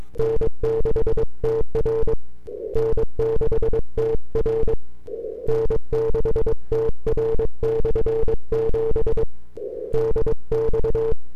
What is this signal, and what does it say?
Morse code, heard as a keyed tone, about 22 wpm, 450 Hz. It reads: N6TR N6TR N6TR/7 DX
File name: n6tr / 12Oct2009-160